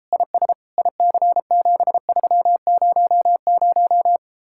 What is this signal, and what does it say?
Morse code, heard as a keyed tone, about 33 wpm, 700 Hz. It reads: IS IC7300